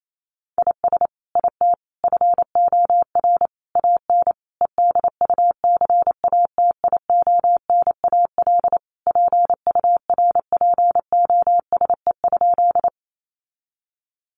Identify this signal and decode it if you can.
Morse code, heard as a keyed tone, about 28 wpm, 700 Hz. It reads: IS IT FOR AN EDUCATIONAL PURPOSE?
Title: Sentences from Top 1200 Words 28wpm